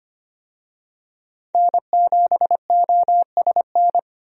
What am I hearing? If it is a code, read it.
Morse code, heard as a keyed tone, about 25 wpm, 690 Hz. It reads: N7OSN
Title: USA Call Signs - 5 Characters - 25wpm